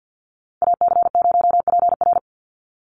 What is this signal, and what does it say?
Morse code, heard as a keyed tone, about 48 wpm, 700 Hz. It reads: AC0PR
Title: NA SOTA Chaser Callsigns 50wpm